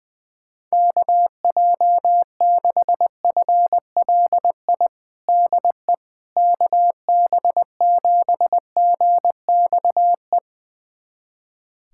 Morse code, heard as a keyed tone, about 20 wpm, 700 Hz. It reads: KJ6FLI DE KB7GXE
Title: USA Callsign DE Callsign - 6 Characters - 20wpm